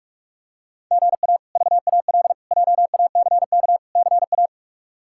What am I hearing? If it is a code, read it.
Morse code, heard as a keyed tone, about 43 wpm, 700 Hz. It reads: GA VAL JACK CA
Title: K1USN SST QSO Practice - 45wpm